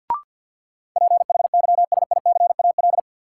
Morse code, heard as a keyed tone, about 51 wpm, 700 Hz. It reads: PHYSICAL
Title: Sentences from Top 1000 Words - Review Part 4 - New Word Drill - 50wpm